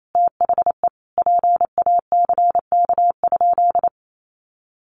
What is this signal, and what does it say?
Morse code, heard as a keyed tone, about 28 wpm, 700 Hz. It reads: THE PACK?
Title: Sentences from Top 1300 Words 28wpm